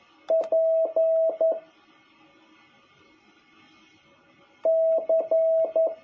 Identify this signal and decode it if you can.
Morse code, heard as a keyed tone, about 11 wpm, 650 Hz. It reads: P C
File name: jh8cbh / 2021cw